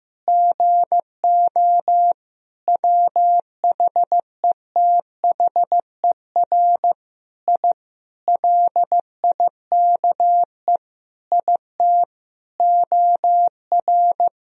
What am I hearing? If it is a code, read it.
Morse code, heard as a keyed tone, about 15 wpm, 700 Hz. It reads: GO WHETHER I LIKE IT OR
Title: Sentences from Top 1000 Words - Review Part 3 - 15wpm